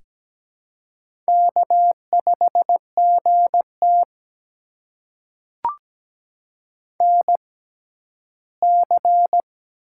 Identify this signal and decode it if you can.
Morse code, heard as a keyed tone, about 17 wpm, spaced as for 12 wpm, 700 Hz. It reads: K5GT NC
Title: USA Call Signs - 6x Character Spacing - 17wpm